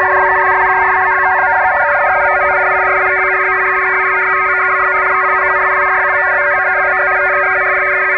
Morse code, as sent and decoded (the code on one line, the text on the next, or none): none